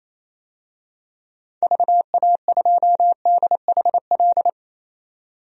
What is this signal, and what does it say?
Morse code, heard as a keyed tone, about 28 wpm, 700 Hz. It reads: VA2DHL
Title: DXCC-100 Call Signs - 6 Characters - 28wpm